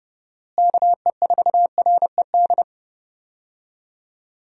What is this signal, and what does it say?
Morse code, heard as a keyed tone, about 30 wpm, 700 Hz. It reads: KE4RED
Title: USA Call Signs - 6 Characters - Repeated 3x - 30wpm